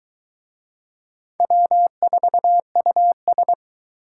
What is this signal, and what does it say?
Morse code, heard as a keyed tone, about 21 wpm, 700 Hz. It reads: W4US